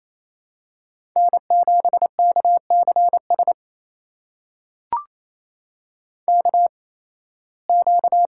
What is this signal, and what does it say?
Morse code, heard as a keyed tone, about 28 wpm, spaced as for 18 wpm, 700 Hz. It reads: N7KCS KQ
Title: USA Call Signs - 5 Characters - 8x Character Spacing - 28wpm